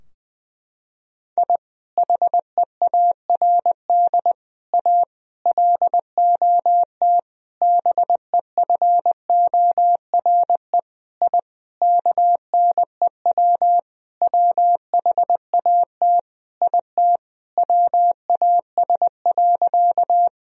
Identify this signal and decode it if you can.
Morse code, heard as a keyed tone, about 20 wpm, 700 Hz. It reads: I HEARD A LOT BEFORE I KNEW WHAT IT WAS.
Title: Sentences from Top 1000 Words - Review Part 3 - 20wpm